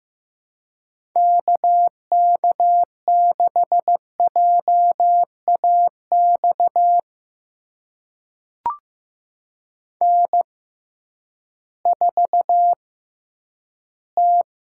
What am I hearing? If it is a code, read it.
Morse code, heard as a keyed tone, about 15 wpm, spaced as for 10 wpm, 700 Hz. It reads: KK6JAX N4T
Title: USA Call Signs - 6x Character Spacing - 15wpm